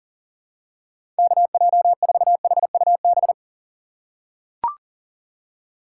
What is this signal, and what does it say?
Morse code, heard as a keyed tone, about 40 wpm, 700 Hz. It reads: KJ4HUB
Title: USA Call Signs - 6 Characters - 6x Character Spacing - 40wpm